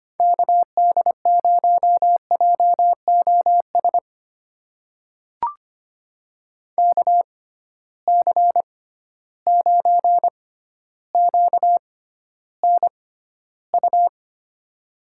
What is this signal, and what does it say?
Morse code, heard as a keyed tone, about 25 wpm, spaced as for 13 wpm, 700 Hz. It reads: KD0JOS KC9QNU
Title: USA Call Signs - 6 Characters - 6x Character Spacing - 25wpm